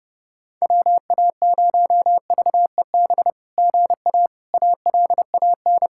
Morse code, heard as a keyed tone, about 30 wpm, 700 Hz. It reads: WA0VEB GA ALAN
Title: SOTA QSO Practice 30wpm